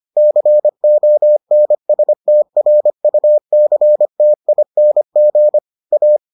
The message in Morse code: CONSTRUCTING A

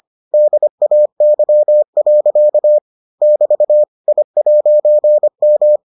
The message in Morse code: DAY. <BT> I'M

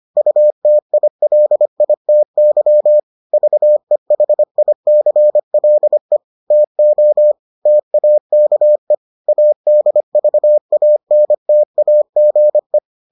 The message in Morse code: UTILITY VEHICLE TO TAKE ADVANTAGE

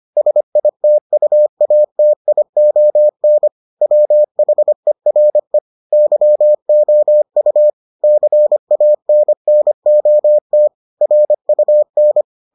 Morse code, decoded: SITUATION WHERE YOU CANNOT RUN

